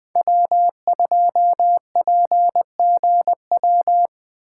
Morse code, heard as a keyed tone, about 24 wpm, 700 Hz. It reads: W2PGW
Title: Speed Racing - USA 1x3 Call Signs 20wpm